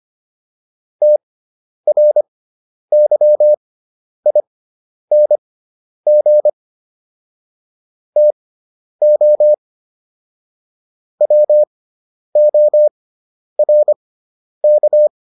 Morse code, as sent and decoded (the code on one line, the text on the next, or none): - .-. -.-- .. -. --. / - --- / .-- --- .-. -.-
TRYING TO WORK